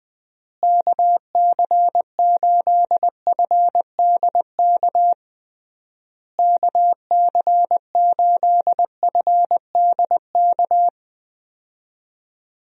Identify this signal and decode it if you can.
Morse code, heard as a keyed tone, about 20 wpm, 700 Hz. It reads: KC8FDK KC8FDK
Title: USA Call Signs - 6 Characters - Repeated 3x - 20wpm